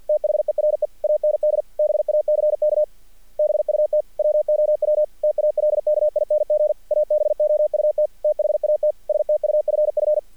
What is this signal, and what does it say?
Morse code, heard as a keyed tone, about 46 wpm, 610 Hz. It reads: THERE AND BACK BUT WOW TALKING ABOUT THAT STUFF
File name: archived / CRSnet